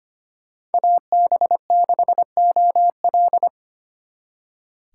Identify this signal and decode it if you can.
Morse code, heard as a keyed tone, about 33 wpm, 700 Hz. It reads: AB6OL